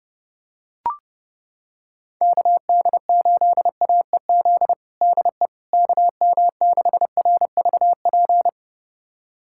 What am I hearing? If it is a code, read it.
Morse code, heard as a keyed tone, about 30 wpm, 700 Hz. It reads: KD8AEZ DE KM6RVP
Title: USA Callsign DE Callsign - 6 Characters - 30wpm